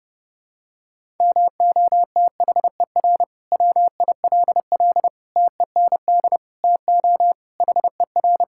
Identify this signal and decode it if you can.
Morse code, heard as a keyed tone, about 32 wpm, 700 Hz. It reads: MOTHER WILL TEND TO HER
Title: Sentences from Top 1000 Words - Review Part 5 - 30wpm